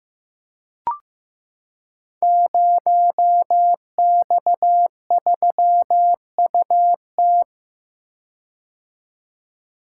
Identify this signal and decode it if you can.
Morse code, heard as a keyed tone, about 15 wpm, 700 Hz. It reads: 0X3UT
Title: Code Groups - 5 Letter:Numbers - 15wpm